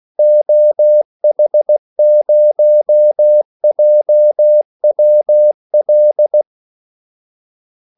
Morse code, heard as a keyed tone, about 16 wpm, 600 Hz. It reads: OH0JWL
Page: Kaldesignaler 16 wpm